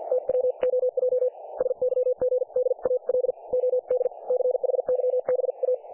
Morse code, heard as a keyed tone, about 50 wpm, spaced as for 43 wpm, 500 Hz. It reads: TWOJ SYGDAL OD CHYBA
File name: dlazaawansowanych / nagraniazpasma / websdr recording 3564.7kHz